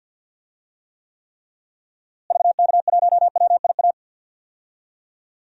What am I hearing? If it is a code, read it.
Morse code, heard as a keyed tone, about 53 wpm, 700 Hz. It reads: VK1WIA
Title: DXCC-100 Call Signs - 6 Characters - Repeated 3x - 50wpm